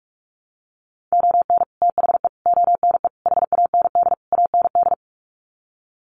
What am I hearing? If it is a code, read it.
Morse code, heard as a keyed tone, about 44 wpm, 700 Hz. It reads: ON THE ONE HAND AND
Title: Sets of 5 Words Encore2 45wpm